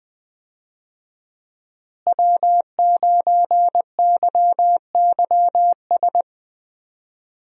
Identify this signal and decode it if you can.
Morse code, heard as a keyed tone, about 20 wpm, 700 Hz. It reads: W9YYS
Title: USA Call Signs - 5 Characters - Repeated 2x - 20wpm